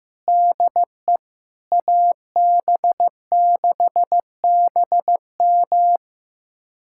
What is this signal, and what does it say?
Morse code, heard as a keyed tone, about 16 wpm, 700 Hz. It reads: DE AB6BM
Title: USA Callsign DE Callsign 15wpm